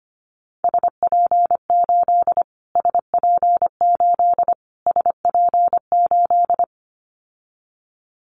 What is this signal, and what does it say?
Morse code, heard as a keyed tone, about 25 wpm, 700 Hz. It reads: SP8 SP8 SP8
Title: Callsign Prefixes for Top 25 DXCC - Repeated 3x - 25wpm